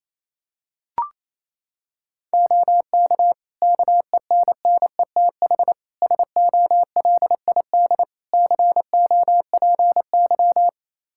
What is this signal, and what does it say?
Morse code, heard as a keyed tone, about 28 wpm, 700 Hz. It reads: OK KENNETH SOLID COPY
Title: Practice QSO - Exchange 3 - More Info - 28wpm